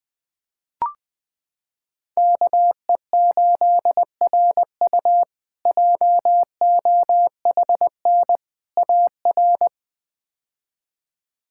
Morse code, heard as a keyed tone, about 20 wpm, 700 Hz. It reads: KE8RU JOHN AR
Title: K1USN SST QSO Practice - 20wpm